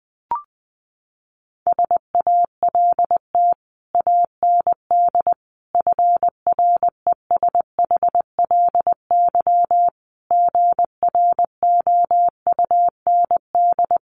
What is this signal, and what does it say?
Morse code, heard as a keyed tone, about 20 wpm, 700 Hz. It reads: SALT AND FRESHLY GROUND
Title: Sets of 5 Words Encore 20wpm